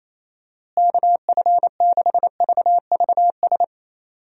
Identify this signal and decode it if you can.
Morse code, heard as a keyed tone, about 28 wpm, 700 Hz. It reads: KF6VVS
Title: USA Call Signs - 6 Characters - Repeated 3x - 28wpm